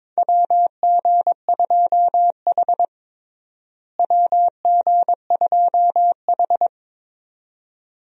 Morse code, heard as a keyed tone, about 22 wpm, 700 Hz. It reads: WG2H WG2H
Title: USA Call Signs - 4 Characters - Repeated 2x - 22wpm